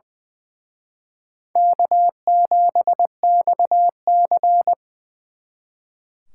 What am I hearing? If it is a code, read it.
Morse code, heard as a keyed tone, about 24 wpm, 700 Hz. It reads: K7XC
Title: Speed Racing - USA 1x2 Call Signs 20wpm